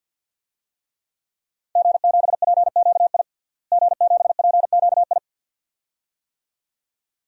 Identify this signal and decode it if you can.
Morse code, heard as a keyed tone, about 53 wpm, 700 Hz. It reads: G7PQI G7PQI
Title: DXCC-100 Call Signs - 5 Characters - Repeated 2x - 50wpm